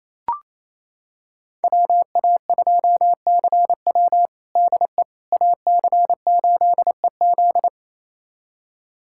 Morse code, heard as a keyed tone, about 28 wpm, 700 Hz. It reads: WA2CW DE AC8EZ
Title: USA Callsign DE Callsign - 5 Characters - 28wpm